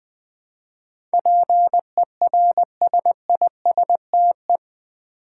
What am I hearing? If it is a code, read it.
Morse code, heard as a keyed tone, about 20 wpm, 700 Hz. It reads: PERSISTE
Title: Build A Word - 4 Letters At A Time - 10-Letter Words - 20wpm